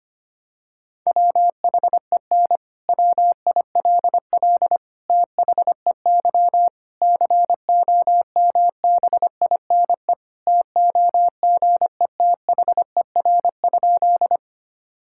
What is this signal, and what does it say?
Morse code, heard as a keyed tone, about 25 wpm, 700 Hz. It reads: WHEN WILL THEY COMBINE TOGETHER?
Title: Sentences from Top 1600 Words 25wpm